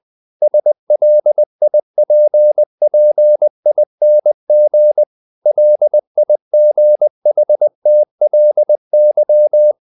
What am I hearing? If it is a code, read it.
Morse code, heard as a keyed tone, about 20 wpm, 600 Hz. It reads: SLIPPING LIGHTLY
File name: the old man and the sea / the old man and the sea part 1 20wpm